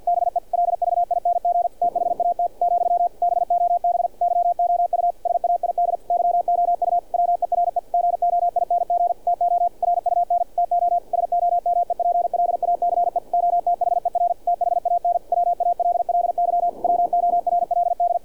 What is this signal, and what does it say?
Morse code, heard as a keyed tone, about 48 wpm, 700 Hz, with radio noise. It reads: BELVING THAT , BOB YOU SAID YOU WERE GOING TO RUN TO SOMEPLACE OTHER THAN WALLY WORLD
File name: archived / CRSnet